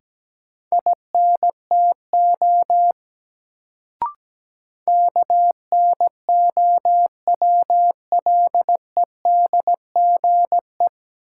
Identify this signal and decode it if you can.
Morse code, heard as a keyed tone, about 17 wpm, 700 Hz. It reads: INTO KNOWLEDGE
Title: Sentences from Top 600 Words - New Word Drill - 17wpm